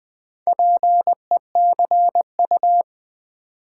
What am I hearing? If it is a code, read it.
Morse code, heard as a keyed tone, about 20 wpm, 700 Hz. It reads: PECU
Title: Build A Word - 4 Letters At A Time - 8-Letter Words - 20wpm